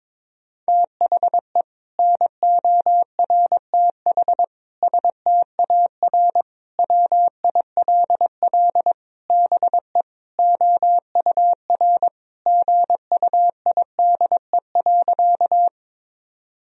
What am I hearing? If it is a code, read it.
Morse code, heard as a keyed tone, about 22 wpm, 700 Hz. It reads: THE NORTH STAR WILL BE OUR GUIDE.
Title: Sentences from Top 1000 Words - Review Part 3 - 22wpm